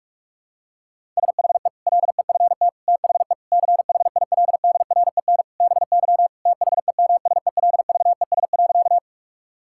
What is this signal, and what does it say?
Morse code, heard as a keyed tone, about 46 wpm, 700 Hz. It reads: SHE LEFT THE CHILDREN BY THEMSELVES.
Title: Sentences from Top 1000 Words - Review Part 5 - 45wpm